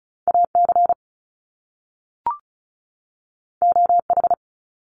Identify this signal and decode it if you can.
Morse code, heard as a keyed tone, about 36 wpm, 700 Hz. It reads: AC OH
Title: Top Two Letter Combinations in Words 35wpm